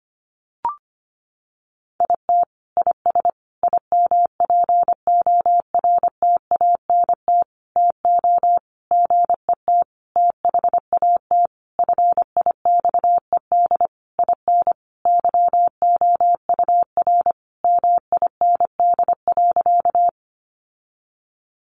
Morse code, heard as a keyed tone, about 25 wpm, 700 Hz. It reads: IT IS IMPORTANT TO GET THAT FIXED IN YOUR MIND.
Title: Sentences from Top 1100 Words 25wpm